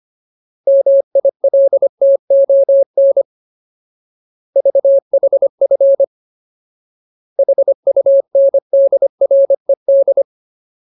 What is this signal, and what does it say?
Morse code, heard as a keyed tone, about 25 wpm, 550 Hz. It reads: MILTON VHF HUNDRED
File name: Ultimate Word List / 25WPM Ultimate